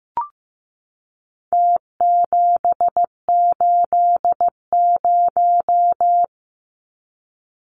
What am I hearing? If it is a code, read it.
Morse code, heard as a keyed tone, about 15 wpm, 700 Hz. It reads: T780
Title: Numbers - Contest Style - 4-Digit 15wpm